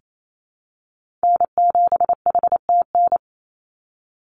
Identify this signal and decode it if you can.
Morse code, heard as a keyed tone, about 28 wpm, 700 Hz. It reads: N7HTN
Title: USA Call Signs 28wpm